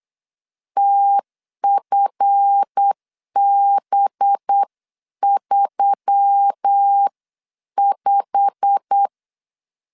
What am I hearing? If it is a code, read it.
Morse code, heard as a keyed tone, about 8 wpm, 800 Hz. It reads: TFB35
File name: CW / DET / 1 9 1500C S B 8wpm 5624